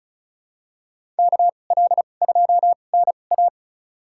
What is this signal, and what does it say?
Morse code, heard as a keyed tone, about 34 wpm, spaced as for 25 wpm, 700 Hz. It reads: KL2NA
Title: USA Call Signs - 5 Characters - 2x Character Spacing - 35wpm